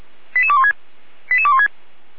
Five-tone selective call (ZVEI), heard as quick runs of five tones, tones of about 70 ms. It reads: 88216 88216